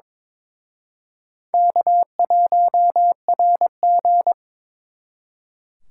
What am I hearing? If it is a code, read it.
Morse code, heard as a keyed tone, about 26 wpm, 700 Hz. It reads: K1RG
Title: Speed Racing - USA Call Signs 22wpm